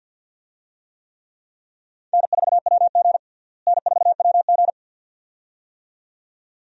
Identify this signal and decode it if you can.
Morse code, heard as a keyed tone, about 47 wpm, 700 Hz. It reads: N4WG N4WG
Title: USA Call Signs - 4 Characters - Repeated 2x - 50wpm